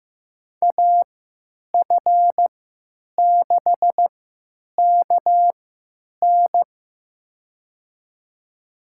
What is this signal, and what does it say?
Morse code, heard as a keyed tone, about 15 wpm, spaced as for 9 wpm, 700 Hz. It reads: AF6KN
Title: USA Call Signs - 5 Characters - 3x Character Spacing - 15wpm